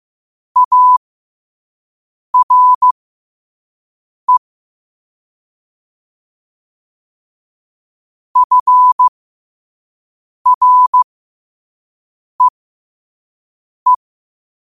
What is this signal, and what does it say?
Morse code, heard as a keyed tone, about 15 wpm, spaced as for 5 wpm, 1000 Hz. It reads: ARE FREE
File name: qotdfiles / 5wpm